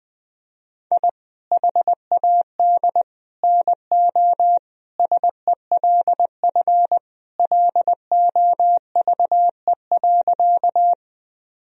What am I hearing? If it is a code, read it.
Morse code, heard as a keyed tone, about 20 wpm, 700 Hz. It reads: I HAD NO SELF LOVE.
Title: Sentences from Top 1000 Words - Review Part 1 - 20wpm